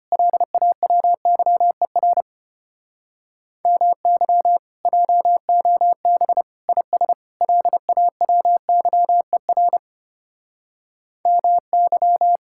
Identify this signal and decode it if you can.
Morse code, heard as a keyed tone, about 33 wpm, 700 Hz. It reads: LAWYER MY JOB IS LAWYER MY